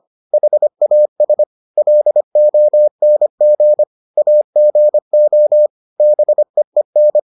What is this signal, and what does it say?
Morse code, heard as a keyed tone, about 25 wpm, 600 Hz. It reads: HAS LONG AGO BEEN